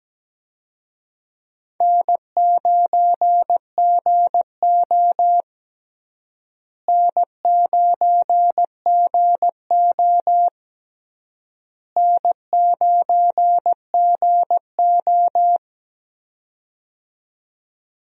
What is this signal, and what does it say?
Morse code, heard as a keyed tone, about 17 wpm, 700 Hz. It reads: N9GO N9GO N9GO
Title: USA Call Signs - 4 Characters - Repeated 3x - 17wpm